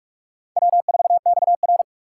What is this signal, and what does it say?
Morse code, heard as a keyed tone, about 44 wpm, 690 Hz. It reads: W4XR
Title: USA Call Signs - 4x Character Spacing - 45wpm